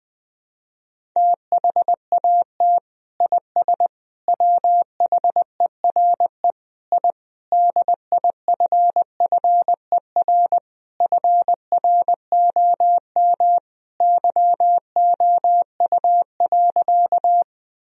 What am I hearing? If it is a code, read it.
Morse code, heard as a keyed tone, about 20 wpm, 700 Hz. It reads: THAT IS WHERE I DIFFER FROM YOU.
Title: Sentences from Top 1000 Words - Review Part 1 - 20wpm